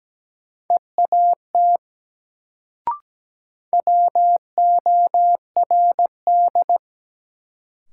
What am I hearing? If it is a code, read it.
Morse code, heard as a keyed tone, about 17 wpm, 700 Hz. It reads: EAT WORD
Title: Sentences from Top 300 Words - New Word Drill - 17wpm